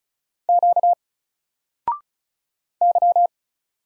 Morse code, heard as a keyed tone, about 36 wpm, 700 Hz. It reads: Q Y
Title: Single Letter-Number 35wpm